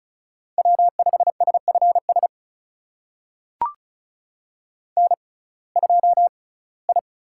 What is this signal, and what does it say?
Morse code, heard as a keyed tone, about 36 wpm, spaced as for 21 wpm, 700 Hz. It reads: W5SFS N2I
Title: USA Call Signs - 5 Characters - 6x Character Spacing - 35wpm